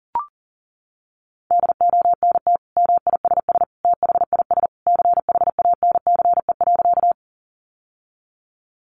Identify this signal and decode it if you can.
Morse code, heard as a keyed tone, about 40 wpm, 700 Hz. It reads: DONT MISS THIS CHANCE.
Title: Sentences from Top 800 Words 40wpm